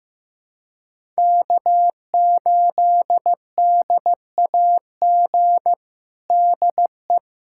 Morse code, heard as a keyed tone, about 15 wpm, 700 Hz. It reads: K8DAG DE
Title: USA Callsign DE Callsign - 5 Characters - 15wpm